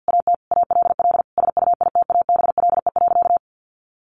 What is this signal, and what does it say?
Morse code, heard as a keyed tone, about 50 wpm, 700 Hz. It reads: AT ALL SUITABLE.
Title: Sentences from Top 1400 Words 50wpm